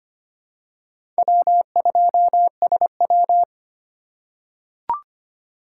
Morse code, heard as a keyed tone, about 25 wpm, 700 Hz. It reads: W2SW
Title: USA Call Signs - 4 Characters - 4x Character Spacing - 25wpm